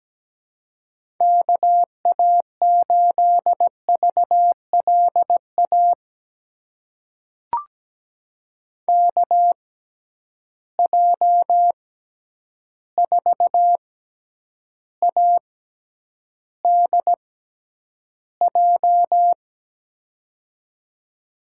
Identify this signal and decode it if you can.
Morse code, heard as a keyed tone, about 17 wpm, spaced as for 9 wpm, 700 Hz. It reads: KA8VLA KJ4ADJ
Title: USA Call Signs - 6 Characters - 6x Character Spacing - 17wpm